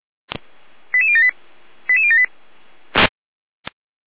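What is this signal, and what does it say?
Five-tone selective call (ZVEI), heard as quick runs of five tones, tones of about 70 ms. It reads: 88078 88078
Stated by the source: ZVEI five-tone selective call received on SDR.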